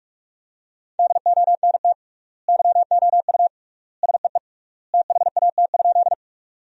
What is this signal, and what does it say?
Morse code, heard as a keyed tone, about 46 wpm, 700 Hz. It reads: DONT YOU SEE THAT?
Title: 3x Word-Spacing - Sentences from Top 300 Words - 45wpm